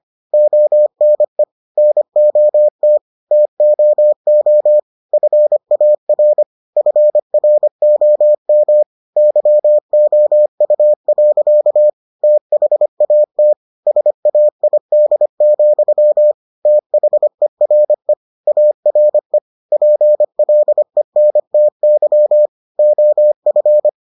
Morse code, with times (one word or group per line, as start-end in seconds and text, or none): 0.3-1.4 ONE
1.8-3.0 NOT
3.3-4.8 TOO
5.1-6.4 FAR
6.8-8.8 FROM
9.2-11.9 YOU.
12.2-13.5 THAT
13.9-16.3 SAID,
16.7-18.1 THERE
18.5-19.4 ARE
19.7-22.5 PLENTY
22.8-23.9 OF